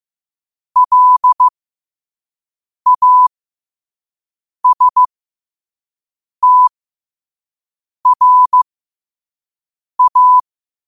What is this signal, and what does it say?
Morse code, heard as a keyed tone, about 15 wpm, spaced as for 5 wpm, 1000 Hz. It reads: LASTRA
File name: qotdfiles / 5wpm